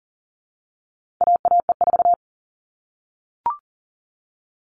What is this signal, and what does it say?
Morse code, heard as a keyed tone, about 40 wpm, 700 Hz. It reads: AAE4
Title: All Cut Numbers - 4-Digit 40wpm